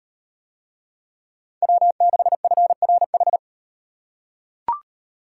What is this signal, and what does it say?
Morse code, heard as a keyed tone, about 38 wpm, 700 Hz. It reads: W6FRH